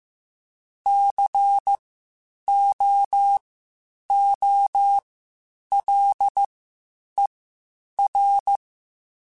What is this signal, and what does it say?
Morse code, heard as a keyed tone, about 15 wpm, spaced as for 8 wpm, 800 Hz. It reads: COOLER
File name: Spaced Characters / Spaced Characters-mp3s / Weather 5 5 sp @13